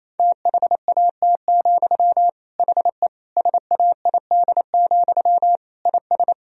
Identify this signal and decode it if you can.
Morse code, heard as a keyed tone, about 27 wpm, 700 Hz. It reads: THAT, HE SAID, IS
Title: Sentences from Top 1000 Words - Review Part 3 - 28wpm